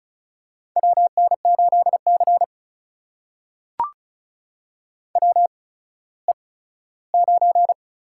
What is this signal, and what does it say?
Morse code, heard as a keyed tone, about 36 wpm, spaced as for 17 wpm, 700 Hz. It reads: WN8C WE9
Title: USA Call Signs - 4 Characters - 8x Character Spacing - 35wpm